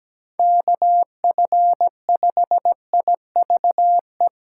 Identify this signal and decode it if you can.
Morse code, heard as a keyed tone, about 17 wpm, 700 Hz. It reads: KF5IVE
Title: USA Call Signs - 6 Characters - 6x Character Spacing - 17wpm